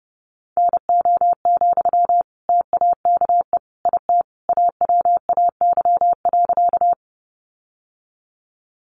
Morse code, heard as a keyed tone, about 30 wpm, 700 Hz. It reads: NO, TAKE IT AWAY.
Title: Sentences from Top 600 Words 30wpm